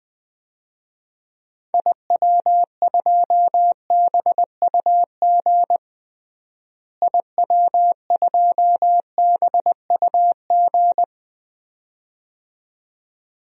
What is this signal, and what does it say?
Morse code, heard as a keyed tone, about 20 wpm, 700 Hz. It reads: IW2BUG IW2BUG
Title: DXCC-100 Call Signs - 6 Characters - Repeated 2x - 20wpm